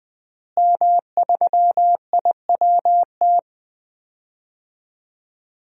Morse code, heard as a keyed tone, about 21 wpm, 700 Hz. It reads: M3IWT
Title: DXCC-100 Call Signs - 5 Characters - Repeated 3x - 20wpm